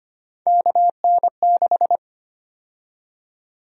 Morse code, heard as a keyed tone, about 25 wpm, 700 Hz. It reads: KN6
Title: USA Call Sign Prefixes - Repeated 2x - 25wpm